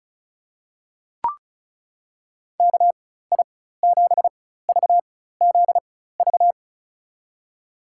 Morse code, heard as a keyed tone, about 34 wpm, spaced as for 16 wpm, 700 Hz. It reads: KI7VZV
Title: USA Call Signs - 6 Characters - 4x Character Spacing - 35wpm